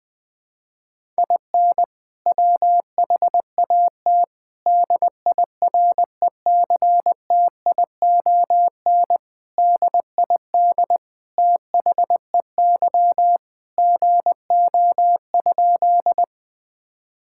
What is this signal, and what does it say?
Morse code, heard as a keyed tone, about 21 wpm, 700 Hz. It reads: IN WHAT DIRECTION DID THEY GO?
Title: Sentences from Top 1000 Words - Review Part 5 - 20wpm